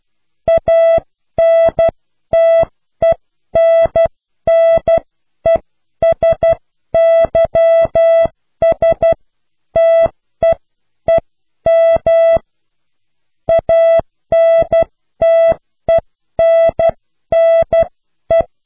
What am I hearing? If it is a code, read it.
Morse code, heard as a keyed tone, about 12 wpm, spaced as for 10 wpm, 660 Hz. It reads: ANTENNESYSTEEM ANTENNE